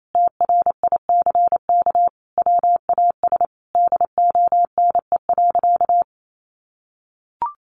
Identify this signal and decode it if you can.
Morse code, heard as a keyed tone, about 28 wpm, 700 Hz. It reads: TRICK WAS DONE.
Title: Sentences from Top 1500 Words 28wpm